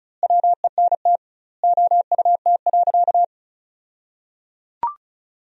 Morse code, heard as a keyed tone, about 34 wpm, 700 Hz. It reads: WENT OUT.
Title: 2x Word-Spacing - Sentences from Top 300 Words - 35wpm